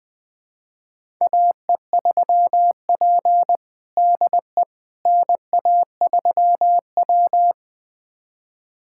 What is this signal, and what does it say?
Morse code, heard as a keyed tone, about 20 wpm, 700 Hz. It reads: AE3P DE NA3W
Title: USA Callsign DE Callsign - 4 Characters - 20wpm